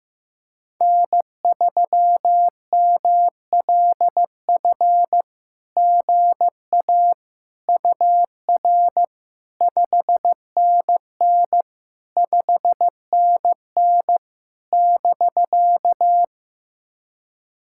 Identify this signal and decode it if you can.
Morse code, heard as a keyed tone, about 14 wpm, 700 Hz. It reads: N3MLF GA UR 5NN 5NN <BK>
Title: SOTA QSO Practice 15wpm